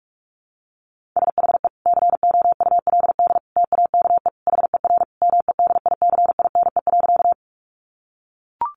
Morse code, heard as a keyed tone, about 44 wpm, 700 Hz. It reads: SHE COULD TAKE HER MEDICINE.
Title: Sentences from Top 900 Words 45wpm